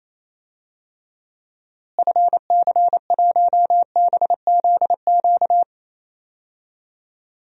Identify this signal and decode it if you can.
Morse code, heard as a keyed tone, about 28 wpm, 700 Hz. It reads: FC1BZQ
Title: DXCC-100 Call Signs - 6 Characters - Repeated 2x - 28wpm